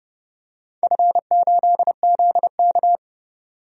Morse code, heard as a keyed tone, about 30 wpm, 700 Hz. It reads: F8ZK
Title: DXCC-100 Call Signs - 4 Characters - 30wpm